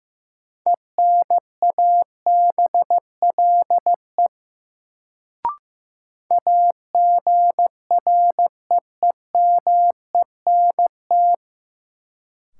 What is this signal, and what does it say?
Morse code, heard as a keyed tone, about 15 wpm, 700 Hz. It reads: ENABLE AGREEMENT
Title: Sentences from Top 1500 Words - New Word Drill - 15wpm